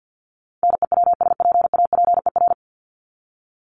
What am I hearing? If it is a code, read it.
Morse code, heard as a keyed tone, about 50 wpm, 700 Hz. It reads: NEWSPAPER
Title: Top 751-1000 Words 50wpm